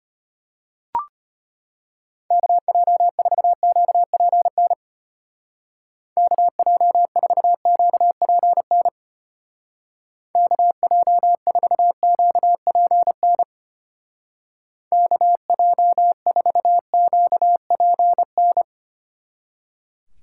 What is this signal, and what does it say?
Morse code, heard as a keyed tone, about 32 wpm, 700 Hz. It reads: KJ4QPN KJ4QPN KJ4QPN KJ4QPN